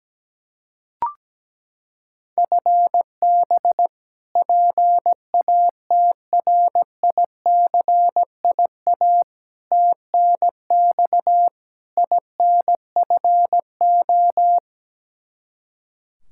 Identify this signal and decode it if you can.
Morse code, heard as a keyed tone, about 17 wpm, 700 Hz. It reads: FB PATRICIA TNX INFO
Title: Practice QSO - Exchange 3 - More Info - 17wpm